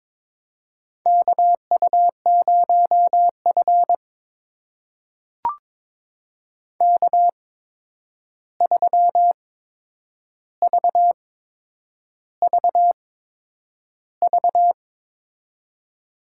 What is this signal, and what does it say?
Morse code, heard as a keyed tone, about 22 wpm, spaced as for 9 wpm, 700 Hz. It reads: KU0F K3VVV
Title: USA Call Signs - 8x Character Spacing - 22wpm